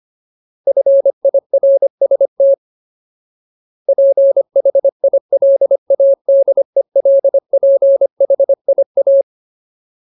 Morse code, heard as a keyed tone, about 25 wpm, 550 Hz. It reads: FIRST PHILADELPHIA
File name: Ultimate Word List / 25WPM Ultimate